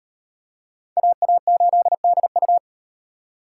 Morse code, heard as a keyed tone, about 38 wpm, 700 Hz. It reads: AA8DU